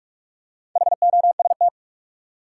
Morse code, heard as a keyed tone, about 46 wpm, 700 Hz. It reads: HOST
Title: Four Letter Words - 45wpm